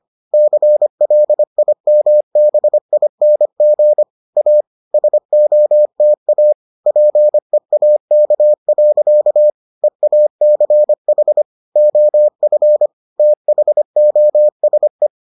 Morse code, heard as a keyed tone, about 25 wpm, 600 Hz. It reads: CLIMBING A SOTA PEAK. EACH OF THOSE